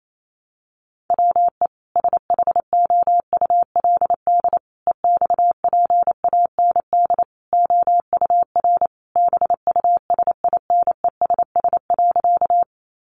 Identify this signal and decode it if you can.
Morse code, heard as a keyed tone, about 28 wpm, 700 Hz. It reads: WE SHOULD EXPAND OUR BUSINESS.
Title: Sentences from Top 1400 Words 28wpm